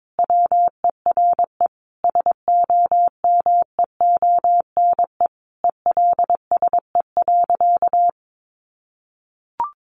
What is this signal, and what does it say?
Morse code, heard as a keyed tone, about 22 wpm, 700 Hz. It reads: WERE SOMEONE ELSE.
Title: Sentences from Top 800 Words 22wpm